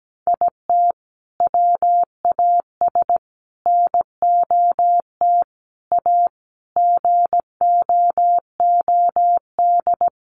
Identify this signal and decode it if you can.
Morse code, heard as a keyed tone, about 17 wpm, 700 Hz. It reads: IT WAS NOT A GOOD
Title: Sentences from Top 1100 Words 17wpm